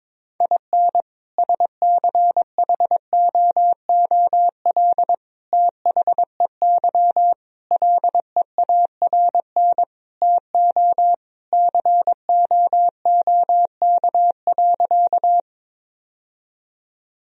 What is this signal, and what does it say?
Morse code, heard as a keyed tone, about 22 wpm, 700 Hz. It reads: IN SCHOOL THEY LEARN TO COOK.
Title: Sentences from Top 1000 Words - Review Part 4 - 22wpm